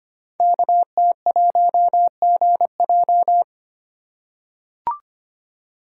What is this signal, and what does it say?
Morse code, heard as a keyed tone, about 25 wpm, 700 Hz. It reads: KT1GJ
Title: Code Groups - 5 Letter:Numbers - 25wpm